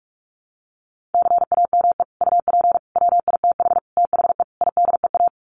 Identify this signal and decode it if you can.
Morse code, heard as a keyed tone, about 44 wpm, 700 Hz. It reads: CAME UP WITH THE IDEA
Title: Sets of 5 Words Encore2 45wpm